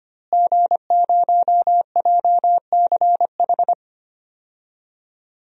Morse code, heard as a keyed tone, about 25 wpm, 700 Hz. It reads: G0JCH
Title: DXCC-100 Call Signs - 5 Characters - Repeated 2x - 25wpm